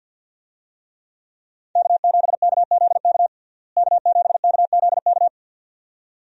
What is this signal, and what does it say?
Morse code, heard as a keyed tone, about 50 wpm, 700 Hz. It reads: K7KZK K7KZK
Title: USA Call Signs - 5 Characters - Repeated 2x - 50wpm